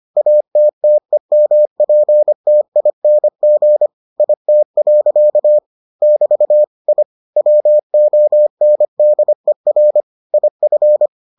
Morse code, heard as a keyed tone, about 25 wpm, 600 Hz. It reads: ATTEMPTING IT. <BT> I WONDER IF